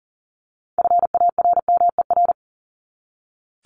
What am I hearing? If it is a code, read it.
Morse code, heard as a keyed tone, about 40 wpm, 700 Hz. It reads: FARMER
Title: Top 1751-2000 Words 40wpm